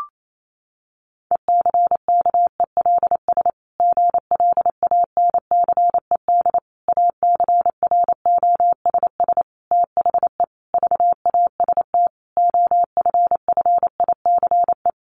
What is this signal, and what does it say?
Morse code, heard as a keyed tone, about 28 wpm, 700 Hz. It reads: ECKELS GLANCED ACROSS THE VAST OFFICE
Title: A Sound Of Thunder 28wpm